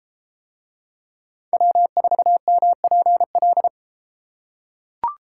Morse code, heard as a keyed tone, about 33 wpm, 700 Hz. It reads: W4MPL